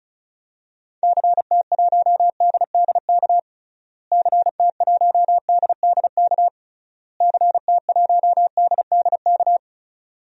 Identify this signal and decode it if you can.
Morse code, heard as a keyed tone, about 36 wpm, 700 Hz. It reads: CT1DDK CT1DDK CT1DDK
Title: DXCC-100 Call Signs - 6 Characters - Repeated 3x - 35wpm